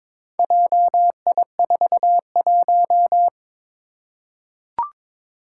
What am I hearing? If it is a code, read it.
Morse code, heard as a keyed tone, about 22 wpm, 700 Hz. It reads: JI41
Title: Four Challenging Letter-Number Combinations 22wpm